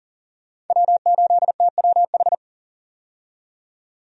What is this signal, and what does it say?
Morse code, heard as a keyed tone, about 40 wpm, 700 Hz. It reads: W8TWH
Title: USA Call Signs - 5 Characters - 40wpm